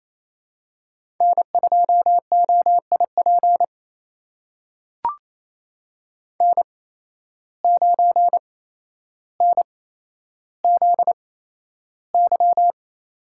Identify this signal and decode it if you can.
Morse code, heard as a keyed tone, about 28 wpm, spaced as for 12 wpm, 700 Hz. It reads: N2OIP N9NZY
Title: USA Call Signs - 5 Characters - 8x Character Spacing - 28wpm